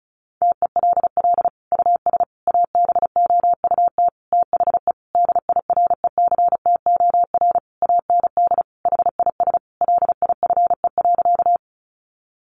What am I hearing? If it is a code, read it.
Morse code, heard as a keyed tone, about 34 wpm, 700 Hz. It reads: TELL US ABOUT THE DIRECTOR AND HIS LIFE.
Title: Sentences from Top 1300 Words 35wpm